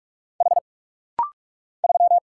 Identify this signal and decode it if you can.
Morse code, heard as a keyed tone, about 46 wpm, 700 Hz. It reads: H 3
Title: Single Letter-Number Alt - Rapid-Fire 45wpm